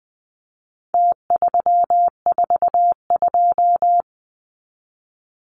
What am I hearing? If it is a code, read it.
Morse code, heard as a keyed tone, about 20 wpm, 700 Hz. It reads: T342
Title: Numbers - Contest Style - 4-Digit 20wpm